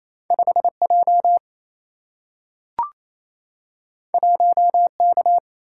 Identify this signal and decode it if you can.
Morse code, heard as a keyed tone, about 28 wpm, 700 Hz. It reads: 5J 1K
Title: Single-Letter Single-Number 28wpm